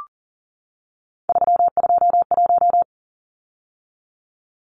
Morse code, heard as a keyed tone, about 40 wpm, 700 Hz. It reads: 321
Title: Numbers - 3-Digit 40wpm